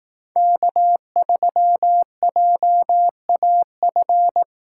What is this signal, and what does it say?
Morse code, heard as a keyed tone, about 18 wpm, 700 Hz. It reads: K3JAF
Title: Speed Racing - USA Call Signs 15wpm